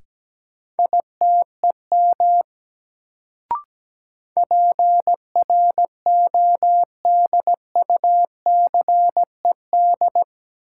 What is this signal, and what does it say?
Morse code, heard as a keyed tone, about 17 wpm, 700 Hz. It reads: ITEM PRODUCED
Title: Sentences from Top 1000 Words - Review Part 6 - New Word Drill - 17wpm